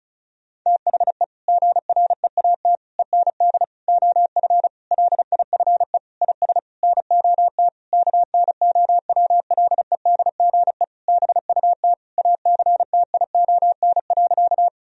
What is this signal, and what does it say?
Morse code, heard as a keyed tone, about 36 wpm, 700 Hz. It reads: THE GREAT END OF LIFE IS NOT KNOWLEDGE BUT ACTION.
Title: Sentences from Top 1000 Words - Review Part 4 - 35wpm